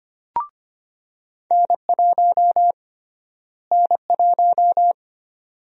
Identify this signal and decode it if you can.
Morse code, heard as a keyed tone, about 25 wpm, 700 Hz. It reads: N1 N1
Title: USA Call Sign Prefixes - Repeated 2x - 25wpm